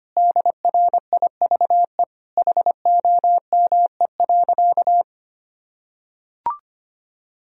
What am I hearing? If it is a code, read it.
Morse code, heard as a keyed tone, about 25 wpm, 700 Hz. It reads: DRIVE HOME.
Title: Sentences from Top 1000 Words - Review Part 2 - 25wpm